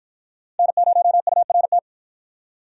Morse code, heard as a keyed tone, about 53 wpm, 700 Hz. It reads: N0URT